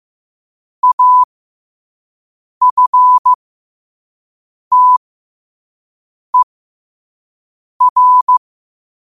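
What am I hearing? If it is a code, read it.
Morse code, heard as a keyed tone, about 15 wpm, spaced as for 5 wpm, 1000 Hz. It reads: AFTER
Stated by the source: Morse sent at 5 wpm.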